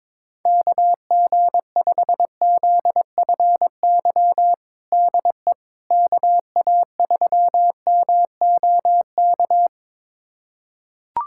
Morse code, heard as a keyed tone, about 22 wpm, 700 Hz. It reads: KG5ZFY DE KA3MOK
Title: USA Callsign DE Callsign - 6 Characters - 22wpm